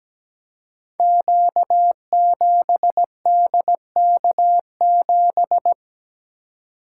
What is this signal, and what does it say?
Morse code, heard as a keyed tone, about 17 wpm, 700 Hz. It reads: Q7DK7
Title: Code Groups - 5 Letter:Numbers - 17wpm